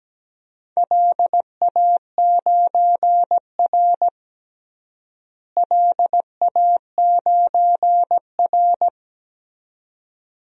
Morse code, heard as a keyed tone, about 17 wpm, 700 Hz. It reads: LA9R LA9R
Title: DXCC-100 Call Signs - 4 Characters - Repeated 2x - 17wpm